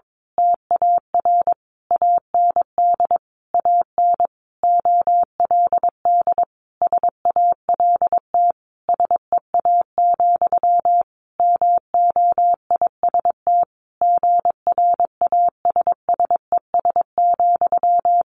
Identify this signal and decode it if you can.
Morse code, heard as a keyed tone, about 22 wpm, 700 Hz. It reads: TAR AND AN OLD SALT SEA, MOIST GRASSES,
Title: A Sound Of Thunder 22wpm